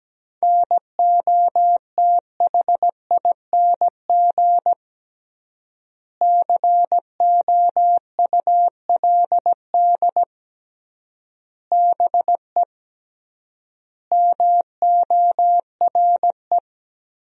Morse code, heard as a keyed tone, about 17 wpm, 700 Hz. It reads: NOTHING COULD BE MORE
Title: 3x Word-Spacing - Sentences from Top 500 Words - 17wpm